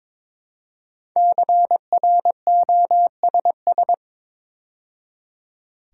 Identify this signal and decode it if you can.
Morse code, heard as a keyed tone, about 22 wpm, 700 Hz. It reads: CROSS
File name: 5 Letter Words - First 3 Letters Given - 22wpm